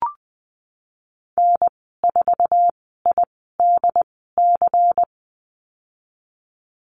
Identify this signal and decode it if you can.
Morse code, heard as a keyed tone, about 20 wpm, spaced as for 15 wpm, 700 Hz. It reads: N4IDC
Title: 128 Lesson 21g - Number 6 - Call Signs 20wpm